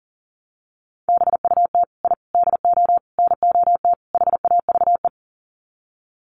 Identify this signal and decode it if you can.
Morse code, heard as a keyed tone, about 40 wpm, 700 Hz. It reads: BUT I DO NOT HAVE
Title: Sets of 5 Words Encore 40wpm